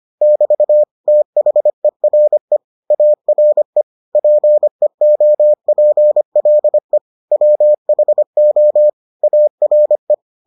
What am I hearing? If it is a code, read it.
Morse code, heard as a keyed tone, about 25 wpm, 600 Hz. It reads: <BT> THERE ARE PEOPLE WHO ARE